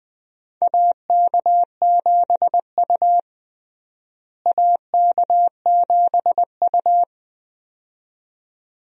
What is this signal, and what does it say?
Morse code, heard as a keyed tone, about 21 wpm, 700 Hz. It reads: AK7U AK7U
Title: USA Call Signs - 4 Characters - Repeated 2x - 20wpm